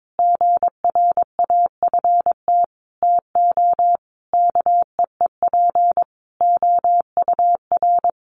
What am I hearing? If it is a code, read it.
Morse code, heard as a keyed tone, about 22 wpm, 700 Hz. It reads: GRAFT TO KEEP OUR
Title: A Sound Of Thunder 22wpm